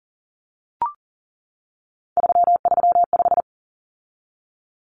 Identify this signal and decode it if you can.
Morse code, heard as a keyed tone, about 40 wpm, 700 Hz. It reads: 335
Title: Numbers - 3-Digit 40wpm